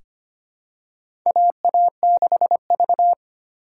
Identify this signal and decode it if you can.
Morse code, heard as a keyed tone, about 25 wpm, 700 Hz. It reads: AA6V
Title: USA Call Signs - 4 Characters - 3x Character Spacing - 25wpm